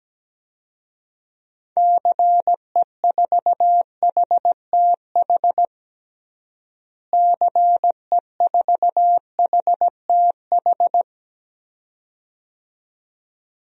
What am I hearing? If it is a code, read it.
Morse code, heard as a keyed tone, about 17 wpm, 700 Hz. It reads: CE4HTH CE4HTH
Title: DXCC-100 Call Signs - 6 Characters - Repeated 2x - 17wpm